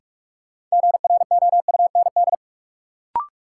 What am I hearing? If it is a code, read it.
Morse code, heard as a keyed tone, about 44 wpm, 700 Hz. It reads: GROUND
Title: Sets of 3 Words - New Word Drill - 45wpm